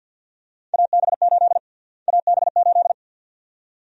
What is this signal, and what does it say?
Morse code, heard as a keyed tone, about 52 wpm, 700 Hz. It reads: AB8 AB8
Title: USA Call Sign Prefixes - Repeated 2x - 50wpm